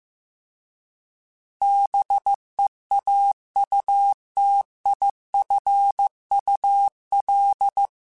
Morse code, heard as a keyed tone, about 15 wpm, 800 Hz. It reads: BEAUTIFUL